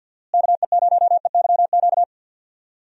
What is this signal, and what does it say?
Morse code, heard as a keyed tone, about 49 wpm, 700 Hz. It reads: KE0EYQ
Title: USA Call Signs - 6 Characters - Repeated 2x - 50wpm